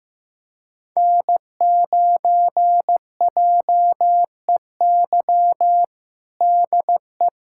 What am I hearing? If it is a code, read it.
Morse code, heard as a keyed tone, about 15 wpm, 700 Hz. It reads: N9JEY DE
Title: USA Callsign DE Callsign - 5 Characters - 15wpm